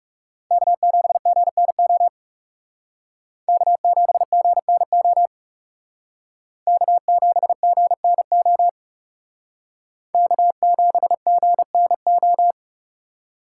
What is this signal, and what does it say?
Morse code, heard as a keyed tone, about 38 wpm, 700 Hz. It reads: K7GNO K7GNO K7GNO K7GNO